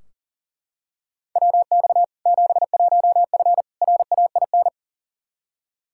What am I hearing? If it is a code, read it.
Morse code, heard as a keyed tone, about 40 wpm, 700 Hz. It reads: WX 71F RAIN
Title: K1USN SST QSO Practice 40wpm